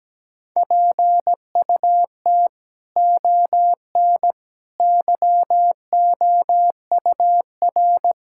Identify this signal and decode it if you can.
Morse code, heard as a keyed tone, about 17 wpm, 700 Hz. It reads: PUT ON YOUR
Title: Sentences from Top 1600 Words 17wpm